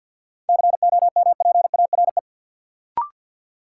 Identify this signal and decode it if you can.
Morse code, heard as a keyed tone, about 53 wpm, 700 Hz. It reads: COMPARE
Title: Sentences from Top 1300 Words - New Word Drill - 50wpm